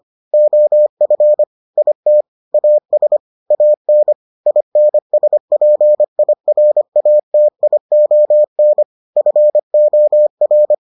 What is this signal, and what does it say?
Morse code, heard as a keyed tone, about 25 wpm, 600 Hz. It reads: OF IT AS AN INSPIRATION FOR